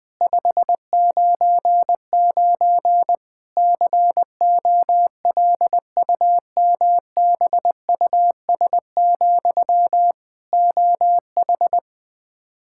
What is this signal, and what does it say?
Morse code, heard as a keyed tone, about 20 wpm, 700 Hz. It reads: 599 COLUMBUS, OH
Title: K1USN SST QSO Practice 20wpm